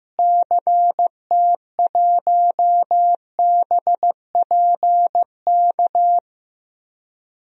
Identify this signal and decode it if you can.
Morse code, heard as a keyed tone, about 15 wpm, 700 Hz. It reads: CT1BPK
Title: DXCC-100 Call Signs - 6 Characters - Repeated 3x - 15wpm